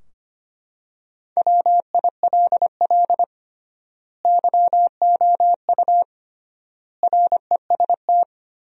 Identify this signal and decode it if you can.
Morse code, heard as a keyed tone, about 25 wpm, 700 Hz. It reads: WILL YOU REST
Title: 3x Word-Spacing - Sentences from Top 500 Words - 25wpm